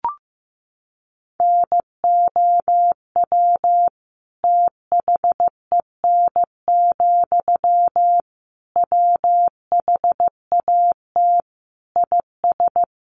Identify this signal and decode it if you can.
Morse code, heard as a keyed tone, about 15 wpm, 700 Hz. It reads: NOW THEN, WHAT IS
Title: Sentences from Top 1100 Words 15wpm